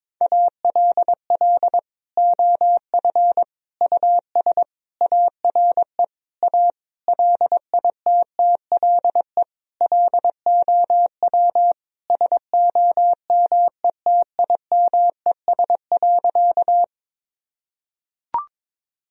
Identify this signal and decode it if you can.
Morse code, heard as a keyed tone, about 22 wpm, 700 Hz. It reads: ALL OF US ARE A LITTLE LOW SOMETIMES.
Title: Sentences from Top 1000 Words - Review Part 3 - 22wpm